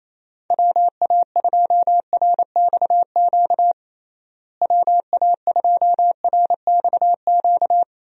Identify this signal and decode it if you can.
Morse code, heard as a keyed tone, about 28 wpm, 700 Hz. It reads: WA2RXQ WA2RXQ
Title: USA Call Signs - 6 Characters - Repeated 2x - 28wpm